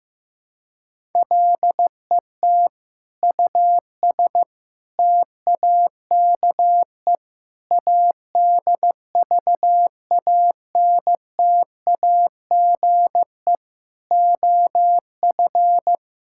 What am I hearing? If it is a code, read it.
Morse code, heard as a keyed tone, about 16 wpm, 700 Hz. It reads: LET US TAKE ADVANTAGE OF
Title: Sentences from Top 1000 Words - Review Part 6 - 15wpm